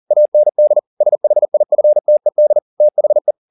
Morse code, heard as a keyed tone, about 40 wpm, 600 Hz. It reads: AND SHIFTED THE